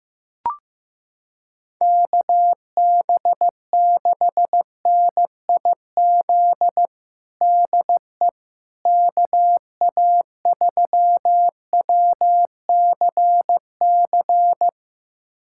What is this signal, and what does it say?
Morse code, heard as a keyed tone, about 16 wpm, 700 Hz. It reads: KB6NIZ DE KA3WCC
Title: USA Callsign DE Callsign - 6 Characters - 15wpm